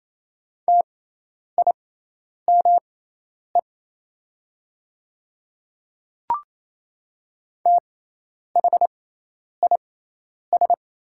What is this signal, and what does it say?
Morse code, heard as a keyed tone, about 27 wpm, spaced as for 10 wpm, 700 Hz. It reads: TIME THIS
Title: 6x Character Spacing - Top 100 Words - 28wpm